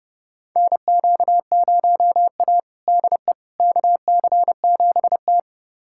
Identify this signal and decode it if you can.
Morse code, heard as a keyed tone, about 30 wpm, 700 Hz. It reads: NQ0A DE KC7T
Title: USA Callsign DE Callsign - 4 Characters - 30wpm